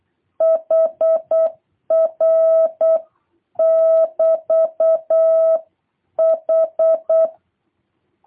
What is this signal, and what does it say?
Morse code, heard as a keyed tone, about 8 wpm, 650 Hz. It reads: HR <BT> H